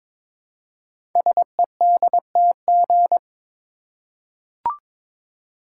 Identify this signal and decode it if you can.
Morse code, heard as a keyed tone, about 22 wpm, 700 Hz. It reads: SEDTG
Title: Code Groups - 5 Letter:Numbers - 22wpm